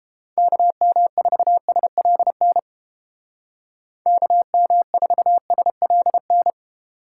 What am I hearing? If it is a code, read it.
Morse code, heard as a keyed tone, about 29 wpm, 700 Hz. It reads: KM4SLN KM4SLN